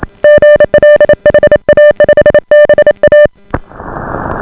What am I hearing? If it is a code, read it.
Morse code, heard as a keyed tone, about 28 wpm, 600 Hz. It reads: GL HA5BA